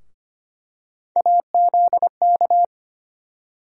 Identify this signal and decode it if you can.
Morse code, heard as a keyed tone, about 25 wpm, 700 Hz. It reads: AZK
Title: USA Call Sign Postfixes - 3 Letters 25wpm